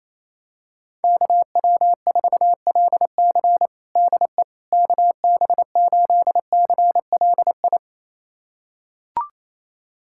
Morse code, heard as a keyed tone, about 27 wpm, 700 Hz. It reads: KW4LC DE KB8CLI
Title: USA Callsign DE Callsign 28wpm